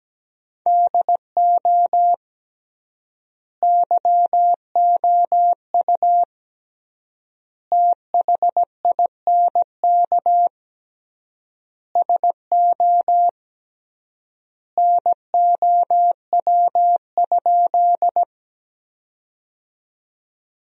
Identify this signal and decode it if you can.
Morse code, heard as a keyed tone, about 17 wpm, 700 Hz. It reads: DO YOU THINK SO NOW?
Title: 3x Word-Spacing - Sentences from Top 200 Words - 17wpm